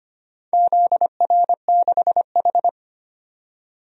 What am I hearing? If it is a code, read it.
Morse code, heard as a keyed tone, about 25 wpm, 700 Hz. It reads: ZR6H
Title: DXCC-100 Call Signs - 4 Characters - 25wpm